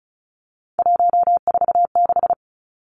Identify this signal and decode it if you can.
Morse code, heard as a keyed tone, about 34 wpm, 700 Hz. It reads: 146
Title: Numbers - 3-Digit 35wpm